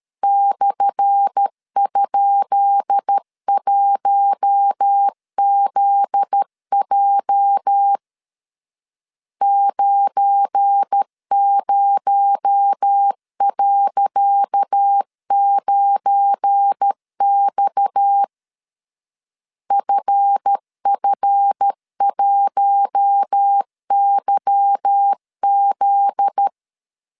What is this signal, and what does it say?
Morse code, heard as a keyed tone, about 13 wpm, 800 Hz. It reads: /?1ZJ 90.9X FF1YZ
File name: LEZIONI / 10 SIGNS / 10C DOT 12wpm 2451